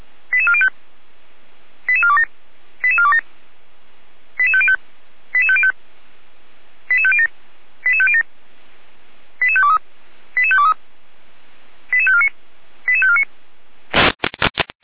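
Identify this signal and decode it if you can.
Five-tone selective call (ZVEI), heard as quick runs of five tones, tones of about 70 ms. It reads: 88405 88427 88427 88505 88505 88507 88507 88523 88523 88549 88549
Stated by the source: ZVEI five-tone selective call received via SDR, Feuerwehr Orlamünde.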